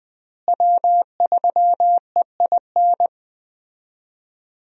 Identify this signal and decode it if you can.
Morse code, heard as a keyed tone, about 20 wpm, 700 Hz. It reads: W3EIN
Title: USA Call Signs - 5 Characters - Repeated 3x - 20wpm